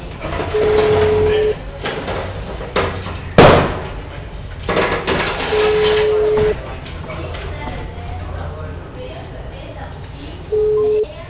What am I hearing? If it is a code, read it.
Ringback tone (Europe, 425 Hz), with speech.
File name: out- -2020- - - .46692